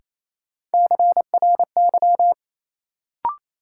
Morse code, heard as a keyed tone, about 28 wpm, 700 Hz. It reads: CRY
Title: Sentences from Top 500 Words - New Word Drill - 28wpm